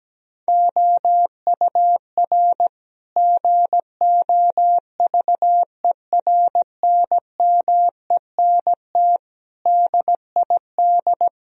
Morse code, heard as a keyed tone, about 17 wpm, 700 Hz. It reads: OUR GOVERNMENT DID
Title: Sentences from Top 1000 Words - Review Part 3 - 17wpm